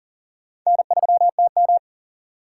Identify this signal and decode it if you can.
Morse code, heard as a keyed tone, about 40 wpm, 700 Hz. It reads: N3TM
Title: USA Call Signs - 4x Character Spacing - 40wpm